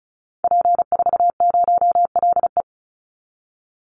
Morse code, heard as a keyed tone, about 34 wpm, 700 Hz. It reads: P40LE
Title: Call Signs - 35wpm